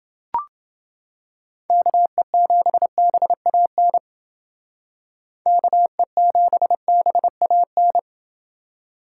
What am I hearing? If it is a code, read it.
Morse code, heard as a keyed tone, about 29 wpm, 700 Hz. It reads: KE7BAN KE7BAN